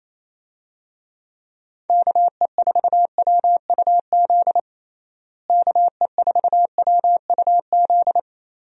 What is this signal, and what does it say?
Morse code, heard as a keyed tone, about 27 wpm, 700 Hz. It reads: KE4WUZ KE4WUZ
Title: USA Call Signs - 6 Characters - Repeated 3x - 28wpm